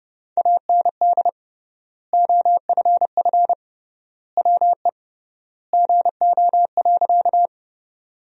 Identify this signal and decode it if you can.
Morse code, heard as a keyed tone, about 29 wpm, 700 Hz. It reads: AND OFF WE GO.
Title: 3x Word-Spacing - Sentences from Top 200 Words - 30wpm